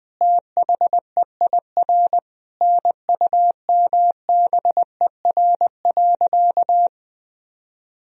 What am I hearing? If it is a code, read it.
Morse code, heard as a keyed tone, about 20 wpm, 700 Hz. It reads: THEIR NUMBER.
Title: Sentences from Top 1000 Words - Review Part 1 - 20wpm